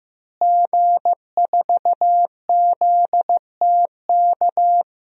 Morse code, heard as a keyed tone, about 15 wpm, 700 Hz. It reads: G4ZTK
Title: DXCC-100 Call Signs - 5 Characters - Repeated 2x - 15wpm